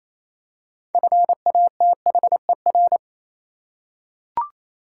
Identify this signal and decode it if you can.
Morse code, heard as a keyed tone, about 27 wpm, 690 Hz. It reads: FATHER
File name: Easy Words - Speed-Racing with Hints - 22wpm